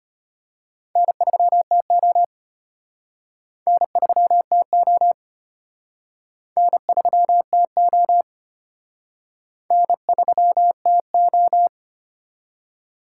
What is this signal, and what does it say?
Morse code, heard as a keyed tone, about 33 wpm, 700 Hz. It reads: N3TO N3TO N3TO N3TO